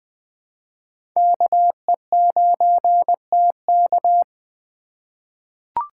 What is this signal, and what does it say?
Morse code, heard as a keyed tone, about 20 wpm, 700 Hz. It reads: KE9TK
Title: USA Call Signs - 5 Characters - 3x Character Spacing - 20wpm